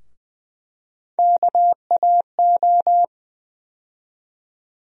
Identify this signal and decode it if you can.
Morse code, heard as a keyed tone, about 21 wpm, 700 Hz. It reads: KAO
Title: USA Call Sign Postfixes - 3 Letters 20wpm